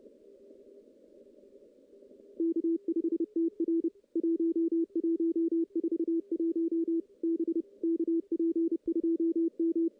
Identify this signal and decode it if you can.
Morse code, heard as a keyed tone, about 30 wpm, 330 Hz, with radio noise. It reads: K5TR 1141 B KP2M
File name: contests / CW-SS-k5tr- -1600